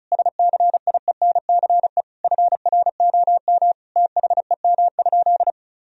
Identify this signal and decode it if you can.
Morse code, heard as a keyed tone, about 34 wpm, 700 Hz. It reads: SCIENCE FROM THEM?
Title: Sentences from Top 1000 Words - Review Part 2 - 35wpm